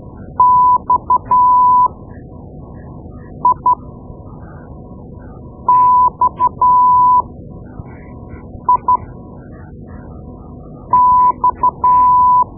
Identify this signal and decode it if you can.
Morse code, heard as a keyed tone, about 12 wpm, spaced as for 4 wpm, 1000 Hz. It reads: XIXIX
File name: Morse Lessons / Morse Lessons / Tape-2B